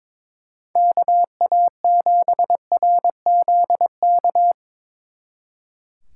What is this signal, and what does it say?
Morse code, heard as a keyed tone, about 24 wpm, 700 Hz. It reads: KA7RZK
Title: Speed Racing - USA 2x3 Call Signs 22wpm